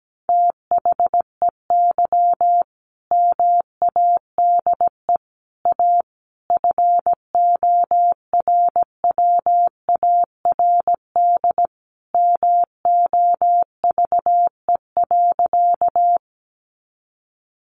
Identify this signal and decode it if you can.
Morse code, heard as a keyed tone, about 17 wpm, 700 Hz. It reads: THEY MADE A FORWARD MOVE.
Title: Sentences from Top 1300 Words 17wpm